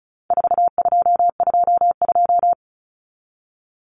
Morse code, heard as a keyed tone, about 34 wpm, 700 Hz. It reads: 4222
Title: Common Cut Numbers - 4-Digit 35wpm